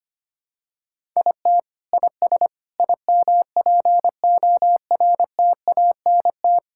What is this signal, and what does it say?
Morse code, heard as a keyed tone, about 25 wpm, 700 Hz. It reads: IT IS IMPORTANT
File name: practice / 25wpm / Sentences with 9-Word Phrases - 25wpm